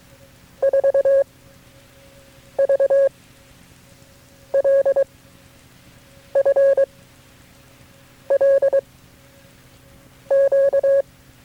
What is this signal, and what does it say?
Morse code, heard as a keyed tone, about 22 wpm, 550 Hz. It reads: 4 V L F L Q
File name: FreeFiles / CW-practice / KB6MT-tape3-side1